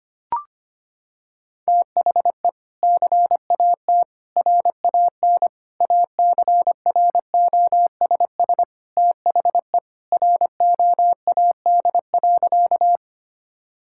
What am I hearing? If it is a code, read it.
Morse code, heard as a keyed tone, about 25 wpm, 700 Hz. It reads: THE CAT RAN ACROSS THE ROAD.
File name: practice / 25wpm / Sentences with 2-Word Phrases - 25wpm